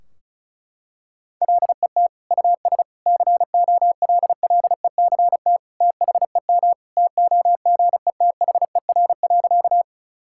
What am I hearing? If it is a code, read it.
Morse code, heard as a keyed tone, about 36 wpm, 700 Hz. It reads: LET US COLLECT THEM TOGETHER.
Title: Sentences from Top 1600 Words 35wpm